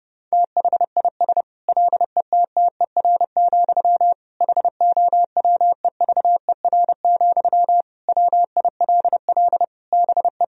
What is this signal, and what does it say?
Morse code, heard as a keyed tone, about 32 wpm, 700 Hz. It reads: THIS LETTER, HOWEVER, WILL BE
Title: Sentences from Top 1000 Words - Review Part 4 - 30wpm